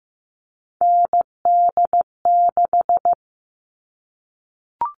Morse code, heard as a keyed tone, about 15 wpm, 700 Hz. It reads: ND6
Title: All Cut Numbers - 3-Digit 15wpm